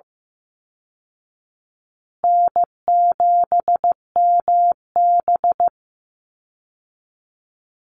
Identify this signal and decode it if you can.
Morse code, heard as a keyed tone, about 15 wpm, 700 Hz. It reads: N7MB
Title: Easy Call Signs Repeated 2x - 15wpm